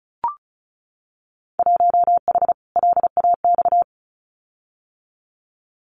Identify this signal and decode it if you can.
Morse code, heard as a keyed tone, about 34 wpm, 700 Hz. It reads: 1H LAX
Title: Winter Field Day Exchanges 35wpm